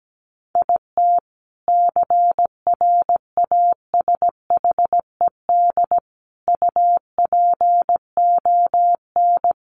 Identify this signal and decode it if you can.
Morse code, heard as a keyed tone, about 17 wpm, 700 Hz. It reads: IT CRASHED UPON
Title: A Sound Of Thunder 17wpm